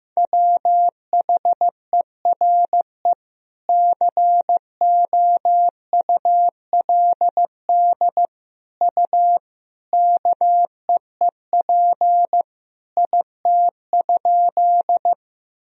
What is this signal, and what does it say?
Morse code, heard as a keyed tone, about 16 wpm, 700 Hz. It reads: WHERE COULD U KEEP IT?
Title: Sentences from Top 300 Words and Abbreviations 15wpm